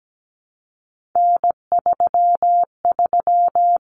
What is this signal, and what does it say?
Morse code, heard as a keyed tone, about 17 wpm, 700 Hz. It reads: N33
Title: Common Cut Numbers - 3-Digit 17wpm